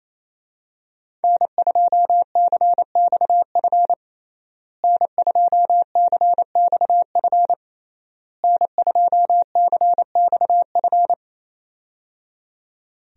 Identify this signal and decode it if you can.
Morse code, heard as a keyed tone, about 28 wpm, 700 Hz. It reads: N2CXF N2CXF N2CXF
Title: USA Call Signs - 5 Characters - Repeated 3x - 28wpm